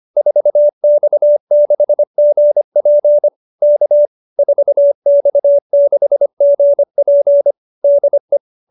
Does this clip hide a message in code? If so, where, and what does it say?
Yes: Morse code from 0.2 to 8.4 s, heard as a keyed tone, about 25 wpm, 590 Hz. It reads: 4X6GP K 4X6GP DE